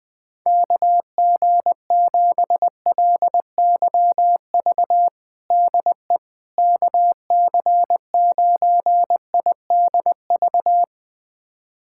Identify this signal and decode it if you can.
Morse code, heard as a keyed tone, about 20 wpm, 700 Hz. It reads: KG7LYV DE KC9IDV
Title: USA Callsign DE Callsign - 6 Characters - 20wpm